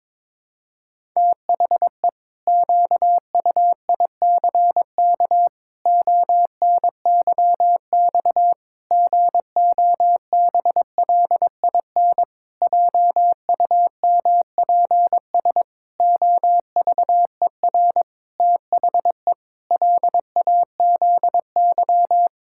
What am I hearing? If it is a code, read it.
Morse code, heard as a keyed tone, about 22 wpm, 700 Hz. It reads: THE QUICK ONYX GOBLIN JUMPS OVER THE LAZY
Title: Pangram Challenge v2 - 22wpm